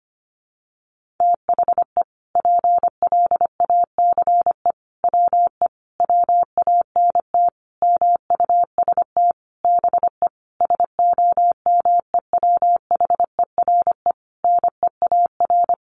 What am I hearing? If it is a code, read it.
Morse code, heard as a keyed tone, about 25 wpm, 700 Hz. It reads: THE PLACE WE WANT MUST BE SOMEWHERE NEAR
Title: Sentences from Top 1400 Words 25wpm